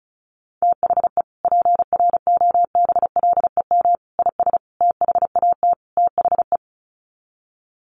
Morse code, heard as a keyed tone, about 34 wpm, 700 Hz. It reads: THE PROBLEM IS THAT THE
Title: Sets of 5 Words Encore2 35wpm